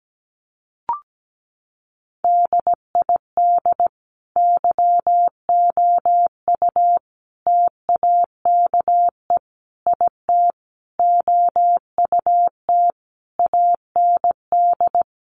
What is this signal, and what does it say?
Morse code, heard as a keyed tone, about 17 wpm, 700 Hz. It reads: DID YOU TAKE IT OUT AND
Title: Sentences from Top 1100 Words 17wpm